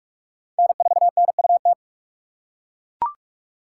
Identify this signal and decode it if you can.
Morse code, heard as a keyed tone, about 43 wpm, 700 Hz. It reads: N4NUT